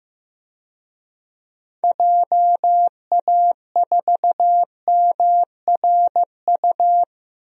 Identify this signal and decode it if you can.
Morse code, heard as a keyed tone, about 15 wpm, 700 Hz. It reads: JA4MRU
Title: DXCC-100 Call Signs - 6 Characters - Repeated 2x - 15wpm